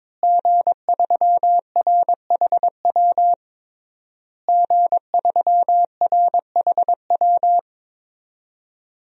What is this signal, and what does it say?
Morse code, heard as a keyed tone, about 22 wpm, 700 Hz. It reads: G3RHW G3RHW
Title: DXCC-100 Call Signs - 5 Characters - Repeated 3x - 22wpm